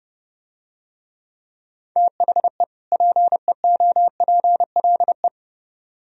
Morse code, heard as a keyed tone, about 29 wpm, 700 Hz. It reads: THE PEOPLE